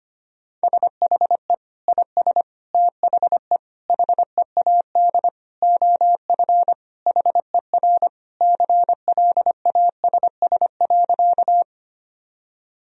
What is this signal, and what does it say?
Morse code, heard as a keyed tone, about 25 wpm, 700 Hz. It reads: SHE IS THE HEAD OF HER CLASS.
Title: Sentences from Top 1000 Words - Review Part 2 - 25wpm